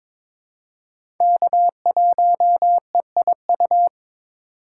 Morse code, heard as a keyed tone, about 21 wpm, 700 Hz. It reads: K1EIU
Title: USA Call Signs - 5 Characters - 2x Character Spacing - 22wpm